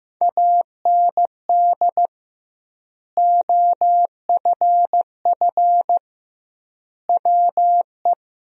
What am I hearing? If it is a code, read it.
Morse code, heard as a keyed tone, about 16 wpm, 700 Hz. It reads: AND OFF WE
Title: 2x Word-Spacing - Sentences from Top 200 Words - 15wpm